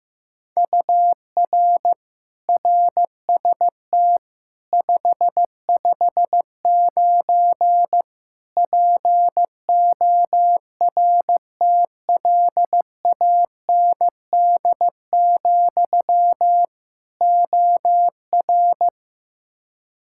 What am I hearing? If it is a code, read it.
Morse code, heard as a keyed tone, about 15 wpm, 700 Hz. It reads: UR RST 559 PORTLAND, OR
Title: K1USN SST QSO Practice 15wpm